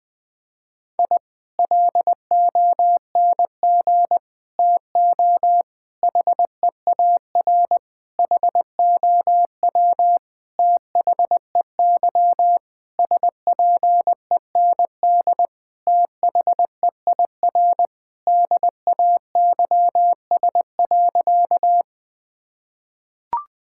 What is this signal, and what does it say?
Morse code, heard as a keyed tone, about 20 wpm, 700 Hz. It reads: I LONG TO HEAR HOW THEY SPEND THEIR DAYS.
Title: Sentences from Top 1000 Words - Review Part 5 - 20wpm